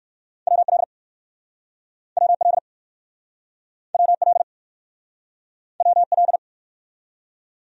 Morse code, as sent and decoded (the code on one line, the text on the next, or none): .-- .-.. / .-- .-.. / .-- .-.. / .-- .-..
WL WL WL WL